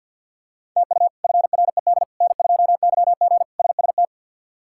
Morse code, heard as a keyed tone, about 51 wpm, 700 Hz. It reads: TU FRED N2YG SST
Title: K1USN SST QSO Practice - 50wpm